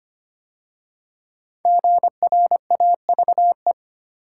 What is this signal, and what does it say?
Morse code, heard as a keyed tone, about 25 wpm, 700 Hz. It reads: GRAVE
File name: 5 Letter Words - First 3 Letters Given - 25wpm